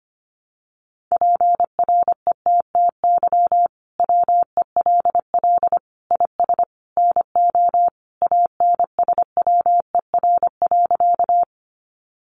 Morse code, heard as a keyed tone, about 25 wpm, 700 Hz. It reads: PRETTY WELL IS NO ANSWER.
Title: Sentences from Top 1000 Words 25wpm